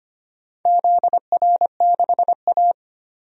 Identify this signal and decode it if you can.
Morse code, heard as a keyed tone, about 25 wpm, 700 Hz. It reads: ZR6A
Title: DXCC-100 Call Signs - 4 Characters - 25wpm